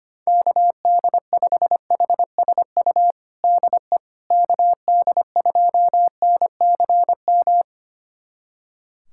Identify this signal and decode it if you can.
Morse code, heard as a keyed tone, about 25 wpm, 700 Hz. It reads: KD5HSU DE KD2NCM
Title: USA Callsign DE Callsign - 6 Characters - 25wpm